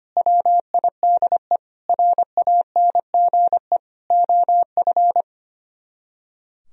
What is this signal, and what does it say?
Morse code, heard as a keyed tone, about 30 wpm, 700 Hz. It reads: WIDE RANGE OF
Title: Speed-Racing - Sets of 4 Words 25wpm